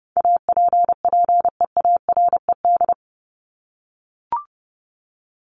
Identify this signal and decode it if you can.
Morse code, heard as a keyed tone, about 30 wpm, 700 Hz. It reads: APPEARED
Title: ED Suffix - Top 100 Words with Base Words 30wpm